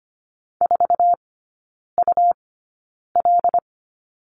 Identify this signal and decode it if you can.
Morse code, heard as a keyed tone, about 25 wpm, spaced as for 9 wpm, 700 Hz. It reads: 4UL
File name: Lesson 17e - Letter u - Three Characters 25wpm